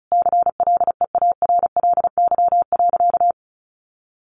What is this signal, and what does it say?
Morse code, heard as a keyed tone, about 34 wpm, 700 Hz. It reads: CLEARLY.
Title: Sentences from Top 1300 Words 35wpm